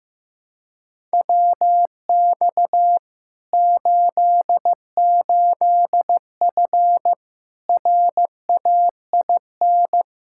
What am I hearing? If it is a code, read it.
Morse code, heard as a keyed tone, about 15 wpm, 700 Hz. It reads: WX 88F RAIN
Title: Practice QSO - Exchange 2 - Info - 15wpm